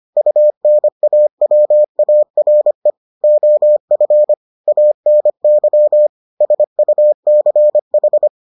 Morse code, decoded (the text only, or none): UNAWARE OF ANY SUCH